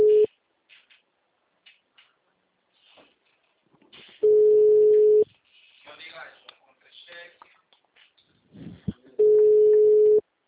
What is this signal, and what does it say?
Ringback tone (Europe, 425 Hz), with speech.
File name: out- -2010- - - .52986